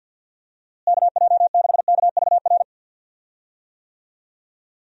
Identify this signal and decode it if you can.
Morse code, heard as a keyed tone, about 47 wpm, 700 Hz. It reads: KJ6KVR
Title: USA Call Signs - 6 Characters - Repeated 2x - 50wpm